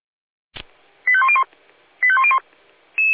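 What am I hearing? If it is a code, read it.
Five-tone selective call (ZVEI), heard as quick runs of five tones, tones of about 70 ms. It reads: 85101 85101
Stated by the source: Received on SDR as ZVEI five-tone selective call 85830.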